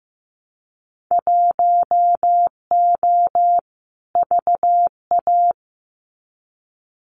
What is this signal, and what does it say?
Morse code, heard as a keyed tone, about 15 wpm, 700 Hz. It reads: 1O VA
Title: Winter Field Day Exchanges 15wpm